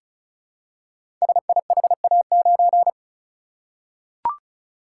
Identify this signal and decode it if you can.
Morse code, heard as a keyed tone, about 36 wpm, 700 Hz. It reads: SIHA9
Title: Code Groups - 5 Letter:Numbers - 35wpm